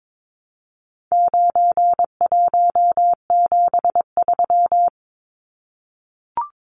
Morse code, heard as a keyed tone, about 22 wpm, 700 Hz. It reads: 9173
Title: Numbers - 4-Digit 22wpm